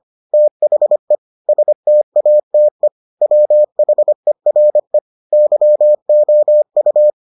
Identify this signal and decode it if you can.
Morse code, heard as a keyed tone, about 25 wpm, 600 Hz. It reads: THE STATE WHERE YOU